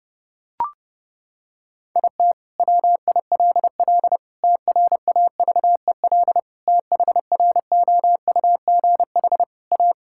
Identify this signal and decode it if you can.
Morse code, heard as a keyed tone, about 30 wpm, 700 Hz. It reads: IT WILL TRAVEL THROUGH A
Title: Sentences from Top 1000 Words - Review Part 6 - 30wpm